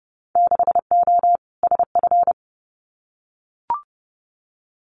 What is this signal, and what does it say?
Morse code, heard as a keyed tone, about 30 wpm, 700 Hz. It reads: 6O SF
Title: Winter Field Day Exchanges Encore 30wpm